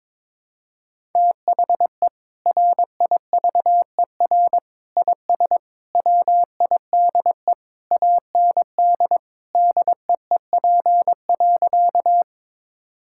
Morse code, heard as a keyed tone, about 22 wpm, 700 Hz. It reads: THE RIVER IS WIDE AND DEEP.
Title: Sentences from Top 1000 Words - Review Part 6 - 22wpm